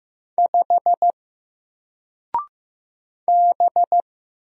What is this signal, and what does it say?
Morse code, heard as a keyed tone, about 15 wpm, 700 Hz. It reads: 5 B
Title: Single Letter-Number 15wpm